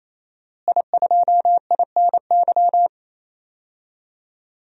Morse code, heard as a keyed tone, about 28 wpm, 700 Hz. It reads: I2INY
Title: DXCC-100 Call Signs - 5 Characters - Repeated 3x - 28wpm